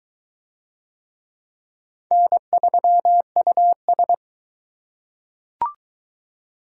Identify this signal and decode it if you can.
Morse code, heard as a keyed tone, about 22 wpm, 700 Hz. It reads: N3US